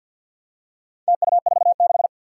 Morse code, heard as a keyed tone, about 53 wpm, 700 Hz. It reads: TU46
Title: Four Challenging Letter-Number Combinations 50wpm